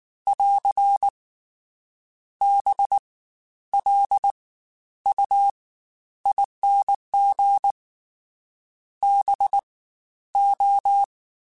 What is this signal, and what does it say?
Morse code, heard as a keyed tone, about 19 wpm, 800 Hz. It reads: <AR> B L U ING B O
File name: Special INGs-mp3s / Special INGs-mp3s / Fast ing @18